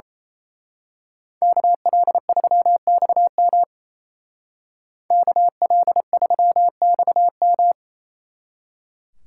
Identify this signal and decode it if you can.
Morse code, heard as a keyed tone, about 29 wpm, 700 Hz. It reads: KL3XM KL3XM
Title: Speed Racing - USA Call Signs 28wpm